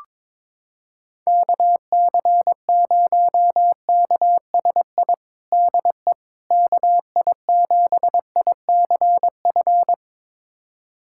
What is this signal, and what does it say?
Morse code, heard as a keyed tone, about 22 wpm, 700 Hz. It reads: KC0KSI DE KI7ICF
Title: USA Callsign DE Callsign - 6 Characters - 22wpm